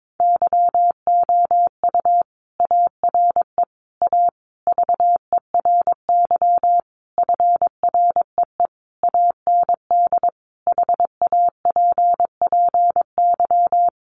Morse code, heard as a keyed tone, about 22 wpm, 700 Hz. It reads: YOU ARE A VERY FREE AND HAPPY
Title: Sentences from Top 1000 Words 22wpm